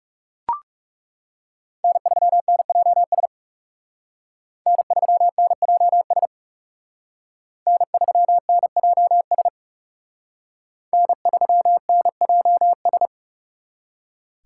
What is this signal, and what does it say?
Morse code, heard as a keyed tone, about 36 wpm, 700 Hz. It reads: N3NJS N3NJS N3NJS N3NJS
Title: Speed Racing - USA 1x3 Call Signs 30wpm